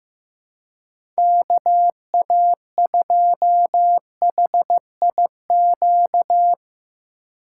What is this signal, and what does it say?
Morse code, heard as a keyed tone, about 15 wpm, 700 Hz. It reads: KA2HIQ
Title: USA Call Signs - 6 Characters - 3x Character Spacing - 15wpm